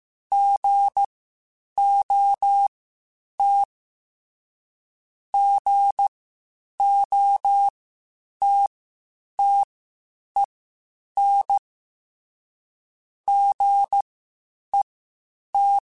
Morse code, heard as a keyed tone, about 15 wpm, spaced as for 8 wpm, 800 Hz. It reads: GOT GOTTEN GET
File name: Short Runs / Short Runs-mp3s / Gotten @13